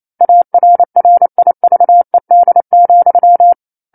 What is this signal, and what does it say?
Morse code, heard as a keyed tone, about 28 wpm, 700 Hz. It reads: ARRIVED,
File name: 09-Chapter-09 28 WPM